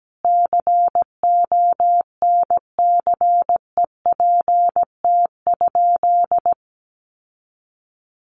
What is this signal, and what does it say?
Morse code, heard as a keyed tone, about 17 wpm, 700 Hz. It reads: CONCEPT?
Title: Sentences from Top 1100 Words 17wpm